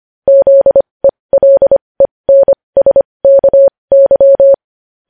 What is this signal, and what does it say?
Morse code, heard as a keyed tone, about 25 wpm, spaced as for 20 wpm, 550 Hz. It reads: ZELENSKY
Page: News Headlines in Morse Code at 20 WPM